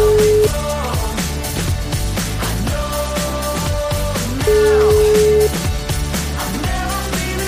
Ringback tone (Europe, 425 Hz), with noise.